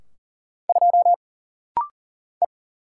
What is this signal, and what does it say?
Morse code, heard as a keyed tone, about 40 wpm, 700 Hz. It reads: 2 E
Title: Single Letter-Number Alt - Rapid-Fire 40wpm